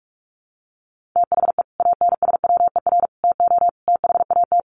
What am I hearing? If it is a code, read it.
Morse code, heard as a keyed tone, about 44 wpm, 700 Hz. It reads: THE ANSWER TO THAT
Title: Sets of 5 Words Encore2 45wpm